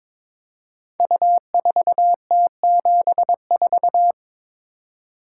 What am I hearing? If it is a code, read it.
Morse code, heard as a keyed tone, about 22 wpm, 690 Hz. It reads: U4T74
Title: Code Groups - 5 Letter:Numbers - 22wpm